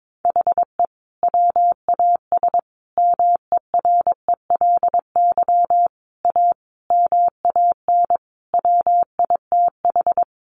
Morse code, heard as a keyed tone, about 22 wpm, 700 Hz. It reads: HE WAS MERELY A MAN WITH
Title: Sentences from Top 1400 Words 22wpm